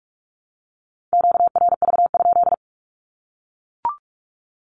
Morse code, heard as a keyed tone, about 44 wpm, 700 Hz. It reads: QRV?
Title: Q-Codes Repeated 3x 45wpm